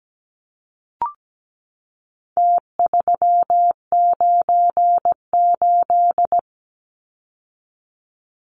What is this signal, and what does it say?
Morse code, heard as a keyed tone, about 17 wpm, 700 Hz. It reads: T398
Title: Numbers - Contest Style - 4-Digit 17wpm